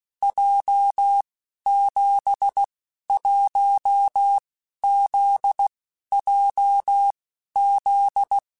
Morse code, heard as a keyed tone, about 16 wpm, 800 Hz. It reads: J 7 1 Z J Z
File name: Basic Letter Groups By7Groups / JZ17 / JZ17@15